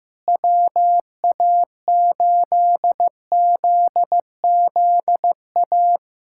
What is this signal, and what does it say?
Morse code, heard as a keyed tone, about 14 wpm, 700 Hz. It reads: WA8ZZA
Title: USA Callsign DE Callsign - 6 Characters - 15wpm